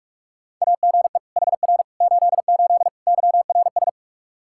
Morse code, heard as a keyed tone, about 46 wpm, 700 Hz. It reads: AGE HR 88 YRS
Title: Practice QSO - Exchange 3 - More Info - 45wpm